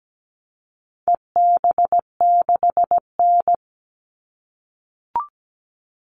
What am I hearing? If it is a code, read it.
Morse code, heard as a keyed tone, about 17 wpm, 700 Hz. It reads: EB6N
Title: All Cut Numbers - 4-Digit 17wpm